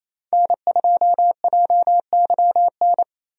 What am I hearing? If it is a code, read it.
Morse code, heard as a keyed tone, about 28 wpm, 700 Hz. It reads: N2JYN
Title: USA Call Signs - 5 Characters - Repeated 3x - 28wpm